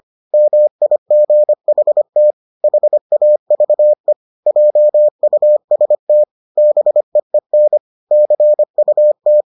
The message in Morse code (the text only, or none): MIGHT HAVE JUST BEEN CUT